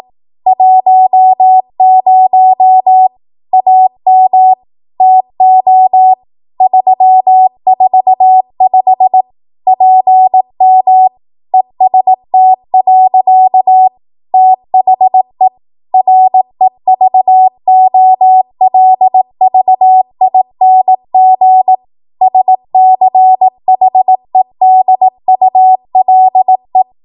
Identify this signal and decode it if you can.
Morse code, heard as a keyed tone, about 18 wpm, 750 Hz. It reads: 10 AM TO 345 PM EST. THE REVOLVING SCHEDULE
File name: Archive / 18 WPM / 18WPM